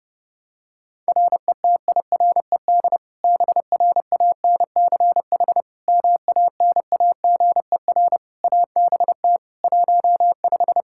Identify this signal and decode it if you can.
Morse code, heard as a keyed tone, about 28 wpm, 700 Hz. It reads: RETIRED BRANCH MANAGER ABT 15
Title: Practice QSO - Exchange 3 - More Info - 30wpm